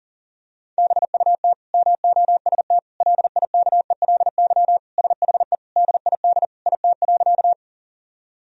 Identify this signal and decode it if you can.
Morse code, heard as a keyed tone, about 40 wpm, 700 Hz. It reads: BUT MOST LIKELY SHE DID IT.
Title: Sentences from Top 1000 Words - Review Part 4 - 40wpm